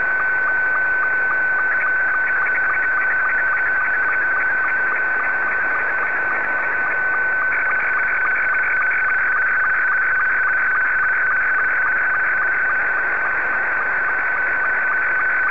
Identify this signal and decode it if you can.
SSTV (slow-scan television), Scottie 2.
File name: sstv / S2-VK7ZGK